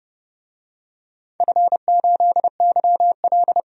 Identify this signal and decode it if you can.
Morse code, heard as a keyed tone, about 30 wpm, 700 Hz. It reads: F8YL
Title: DXCC-100 Call Signs - 4 Characters - 30wpm